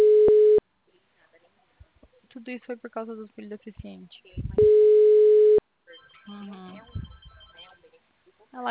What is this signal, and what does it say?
Ringback tone (Europe, 425 Hz), with speech.